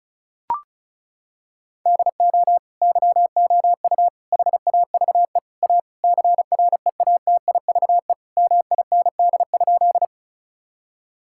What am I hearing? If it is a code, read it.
Morse code, heard as a keyed tone, about 36 wpm, 700 Hz. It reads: DO YOU HAVE A CREATIVE MIND?
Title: Sentences from Top 1000 Words - Review Part 5 - 35wpm